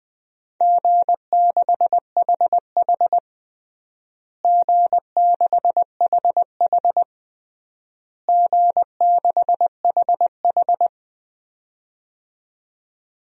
Morse code, heard as a keyed tone, about 20 wpm, 700 Hz. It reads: G6HH G6HH G6HH
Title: DXCC-100 Call Signs - 4 Characters - Repeated 3x - 20wpm